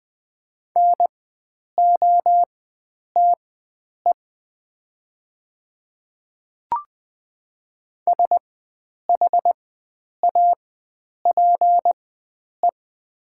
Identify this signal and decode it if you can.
Morse code, heard as a keyed tone, about 20 wpm, spaced as for 9 wpm, 700 Hz. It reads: NOTE SHAPE
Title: 4x Character Spacing - Top 500 Words - 20wpm